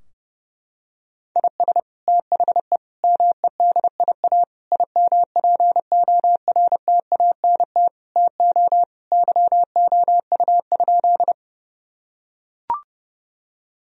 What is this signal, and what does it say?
Morse code, heard as a keyed tone, about 30 wpm, 700 Hz. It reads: IS THE MEDIA IMPORTANT TO YOU?
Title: Sentences from Top 1000 Words - Review Part 4 - 30wpm